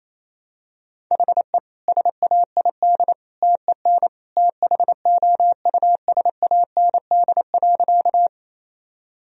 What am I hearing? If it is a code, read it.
Morse code, heard as a keyed tone, about 28 wpm, 700 Hz. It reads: HE SAID TEN THOUSAND.
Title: Sentences from Top 1000 Words - Review Part 3 - 28wpm